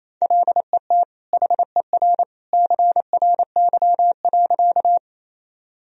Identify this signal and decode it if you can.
Morse code, heard as a keyed tone, about 28 wpm, 700 Hz. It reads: LET HER CRY.
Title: Sentences from Top 1000 Words - Review Part 2 - 28wpm